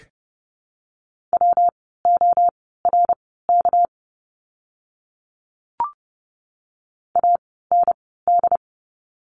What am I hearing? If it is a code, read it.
Morse code, heard as a keyed tone, about 30 wpm, spaced as for 17 wpm, 700 Hz. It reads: WORK AND
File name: course / 30:10wpm / 237 Lesson 38f - Letter q - Words 30wpm